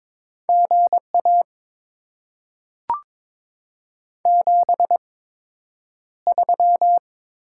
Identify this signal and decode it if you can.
Morse code, heard as a keyed tone, about 22 wpm, spaced as for 9 wpm, 700 Hz. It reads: GA 73
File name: Top 20 QSO Elements - 8x Character Spacing - 22wpm